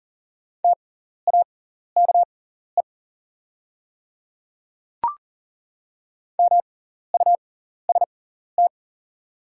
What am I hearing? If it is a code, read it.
Morse code, heard as a keyed tone, about 40 wpm, spaced as for 14 wpm, 690 Hz. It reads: TAKE MUST
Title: 6x Character Spacing - Top 200 Words - 40wpm